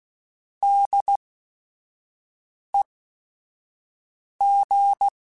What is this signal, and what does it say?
Morse code, heard as a keyed tone, about 16 wpm, spaced as for 5 wpm, 800 Hz. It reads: DEG
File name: Weather Sets / Weather Sets-mp3s / Weather 5 12 @5